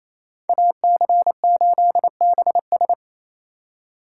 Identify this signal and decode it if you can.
Morse code, heard as a keyed tone, about 37 wpm, 700 Hz. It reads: AC8BS